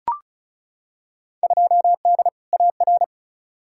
Morse code, heard as a keyed tone, about 36 wpm, 700 Hz. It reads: 2D AR
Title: Field Day Exchanges Encore - 35wpm